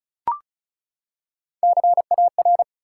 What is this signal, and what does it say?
Morse code, heard as a keyed tone, about 34 wpm, 700 Hz. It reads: CAR
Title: Three Letter Words - 35wpm